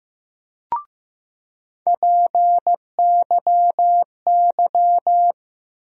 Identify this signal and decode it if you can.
Morse code, heard as a keyed tone, about 15 wpm, 700 Hz. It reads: PYY
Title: USA Call Sign Postfixes - 3 Letters 15wpm